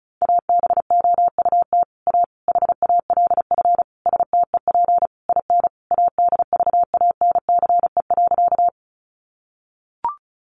Morse code, heard as a keyed tone, about 34 wpm, 700 Hz. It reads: ABOUT A HALF STEP IN ADVANCE.
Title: Sentences from Top 1400 Words 35wpm